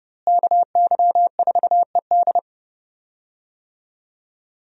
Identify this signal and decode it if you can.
Morse code, heard as a keyed tone, about 30 wpm, 700 Hz. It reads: KY4ED
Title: USA Call Signs - 5 Characters - Repeated 2x - 30wpm